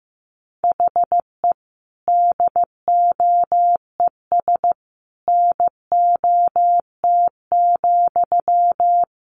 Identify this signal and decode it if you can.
Morse code, heard as a keyed tone, about 15 wpm, 700 Hz. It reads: HE DOES NOT,
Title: Sentences from Top 1200 Words 15wpm